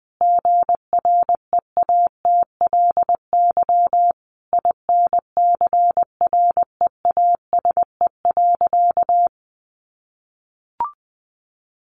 Morse code, heard as a keyed tone, about 19 wpm, 700 Hz. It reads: GREATLY INCREASE.
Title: Sentences from Top 1400 Words 20wpm